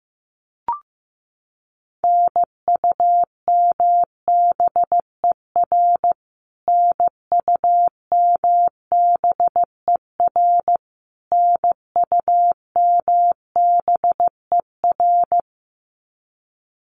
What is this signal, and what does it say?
Morse code, heard as a keyed tone, about 15 wpm, 700 Hz. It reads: NUMBER NUMBER NUMBER
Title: Top 200 Words Repeated 15wpm